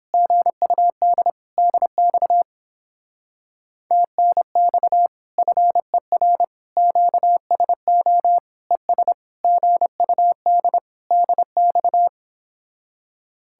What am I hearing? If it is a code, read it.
Morse code, heard as a keyed tone, about 29 wpm, 700 Hz. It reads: GUD DX TNX FER QSO ES GUD DX